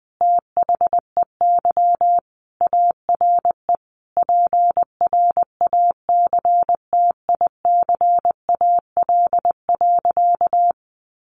Morse code, heard as a keyed tone, about 19 wpm, 700 Hz. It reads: THEY ARE PRACTICAL.
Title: Sentences from Top 1400 Words 20wpm